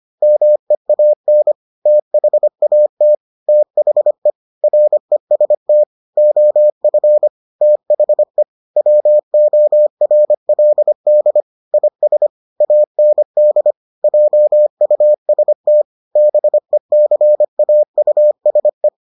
Morse code, heard as a keyed tone, about 25 wpm, 600 Hz. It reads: MEAN THAT THE REST OF THE WORLD IS AND JUST BECAUSE